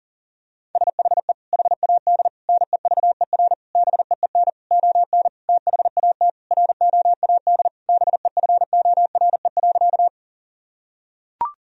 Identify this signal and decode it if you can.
Morse code, heard as a keyed tone, about 40 wpm, 700 Hz. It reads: SHE HAD NEVER BEEN ON THAT ROAD BEFORE.
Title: Sentences from Top 1000 Words - Review Part 2 - 40wpm